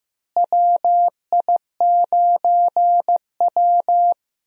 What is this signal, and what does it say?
Morse code, heard as a keyed tone, about 15 wpm, 700 Hz. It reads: WI9W
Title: USA Call Signs - 4 Characters - Repeated 3x - 15wpm